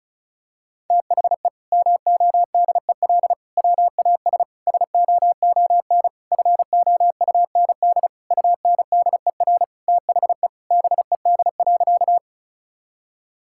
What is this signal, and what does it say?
Morse code, heard as a keyed tone, about 36 wpm, 700 Hz. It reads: THE MODEL WAS SOON FOUND UNDER THE BED.
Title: Sentences from Top 1000 Words - Review Part 5 - 35wpm